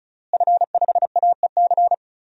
Morse code, heard as a keyed tone, about 36 wpm, 700 Hz. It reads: F5AEC
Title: DXCC-100 Call Signs - 5 Characters - Repeated 3x - 35wpm